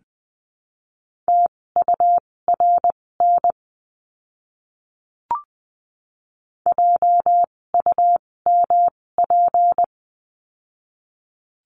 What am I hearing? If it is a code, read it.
Morse code, heard as a keyed tone, about 20 wpm, spaced as for 16 wpm, 700 Hz. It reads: TURN JUMP
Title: 256 Lesson 42a - Words 12-20wpm